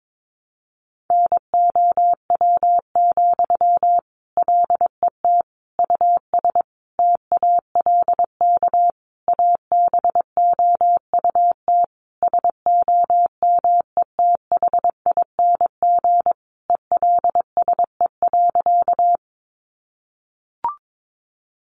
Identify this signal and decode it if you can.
Morse code, heard as a keyed tone, about 22 wpm, 700 Hz. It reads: NOW, LET US TALK ABOUT SOMETHING ELSE.
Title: Sentences from Top 800 Words 22wpm